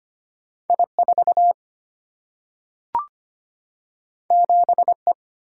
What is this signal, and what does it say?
Morse code, heard as a keyed tone, about 25 wpm, 700 Hz. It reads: I4 7E
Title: Single-Letter Single-Number 25wpm